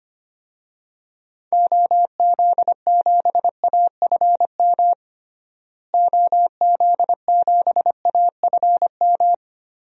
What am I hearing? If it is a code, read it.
Morse code, heard as a keyed tone, about 25 wpm, 690 Hz. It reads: OZ7AFM OZ7AFM
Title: DXCC-100 Call Signs - 6 Characters - Repeated 3x - 25wpm